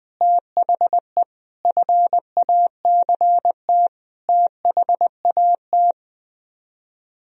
Morse code, heard as a keyed tone, about 21 wpm, 700 Hz. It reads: THE FACT THAT
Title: Speed-Racing - Sets of 4 Words 20wpm